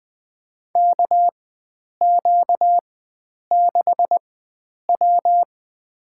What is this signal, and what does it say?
Morse code, heard as a keyed tone, about 20 wpm, spaced as for 9 wpm, 700 Hz. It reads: KQ6W
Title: USA Call Signs - 4 Characters - 4x Character Spacing - 20wpm